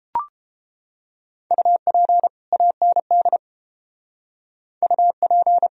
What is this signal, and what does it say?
Morse code, heard as a keyed tone, about 33 wpm, 700 Hz. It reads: UP AND UP